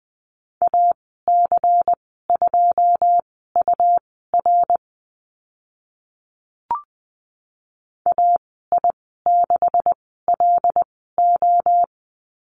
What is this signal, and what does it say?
Morse code, heard as a keyed tone, about 20 wpm, spaced as for 14 wpm, 700 Hz. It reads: AC2UR AI6LO
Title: 128 Lesson 21g - Number 6 - Call Signs 20wpm